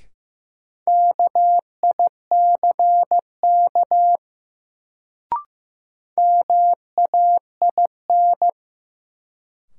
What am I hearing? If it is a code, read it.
Morse code, heard as a keyed tone, about 15 wpm, 700 Hz. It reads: KICK MAIN
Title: Sentences from Top 1300 Words - New Word Drill Alt - 15wpm